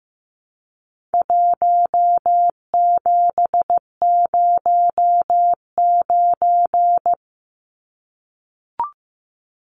Morse code, heard as a keyed tone, about 15 wpm, 700 Hz. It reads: 1709
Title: Numbers - Contest Style - 4-Digit 15wpm